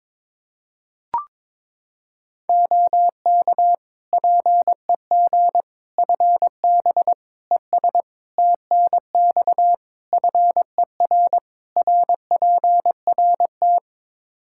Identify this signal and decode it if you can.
Morse code, heard as a keyed tone, about 21 wpm, 700 Hz. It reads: OK PEG FB ES TNX FER RPRT
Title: Practice QSO - Exchange 2 - Info - 22wpm